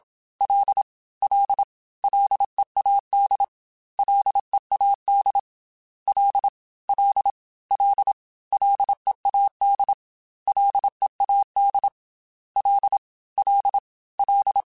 Morse code, heard as a keyed tone, about 27 wpm, 800 Hz. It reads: L L LEAD LEAD L L L LEAD LEAD L L L
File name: Ls / Single L m3s / Single L Prefixe 3x2 Twice @25